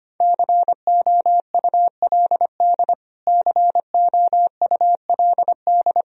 Morse code, heard as a keyed tone, about 25 wpm, 700 Hz. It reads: COULD COULD
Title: Top 100 Words Repeated 25wpm